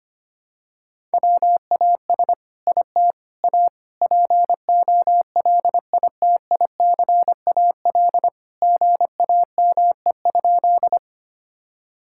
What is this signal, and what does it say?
Morse code, heard as a keyed tone, about 25 wpm, 700 Hz. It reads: WAS IT A POLITICAL GAME?
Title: Sentences from Top 1000 Words - Review Part 5 - 25wpm